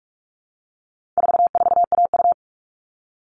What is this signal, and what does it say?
Morse code, heard as a keyed tone, about 44 wpm, 700 Hz. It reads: 44AU
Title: All Cut Numbers - 4-Digit 45wpm